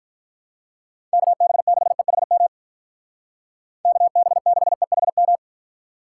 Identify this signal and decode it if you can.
Morse code, heard as a keyed tone, about 44 wpm, 700 Hz. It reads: KB6EHM KB6EHM
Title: Speed Racing - USA 2x3 Call Signs 35wpm